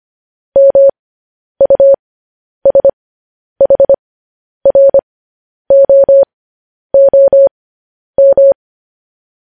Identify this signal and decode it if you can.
Morse code, heard as a keyed tone, about 25 wpm, spaced as for 10 wpm, 550 Hz. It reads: MUSHROOM
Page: news-headlines-in-morse-code-at-10-wpm